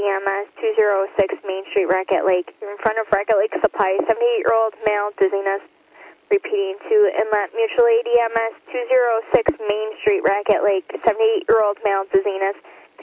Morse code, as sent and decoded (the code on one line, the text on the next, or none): none